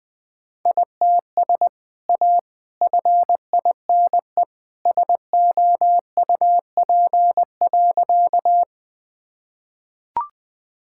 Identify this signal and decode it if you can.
Morse code, heard as a keyed tone, about 20 wpm, 700 Hz. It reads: ITS A FINE SOUP.
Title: Sentences from Top 1000 Words - Review Part 6 - 20wpm